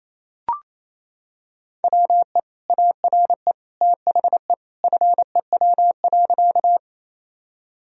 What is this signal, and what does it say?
Morse code, heard as a keyed tone, about 28 wpm, 700 Hz. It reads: WE ARE THE FEW.
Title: Sentences from Top 1000 Words - Review Part 1 - 28wpm